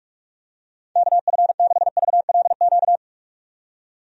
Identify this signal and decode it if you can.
Morse code, heard as a keyed tone, about 46 wpm, 700 Hz. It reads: KF6VLQ
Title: USA Call Signs - 6 Characters - 45wpm